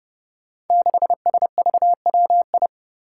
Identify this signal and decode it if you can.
Morse code, heard as a keyed tone, about 30 wpm, 700 Hz. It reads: 6SVWI
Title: Five Challenging Letter-Number Combinations 30wpm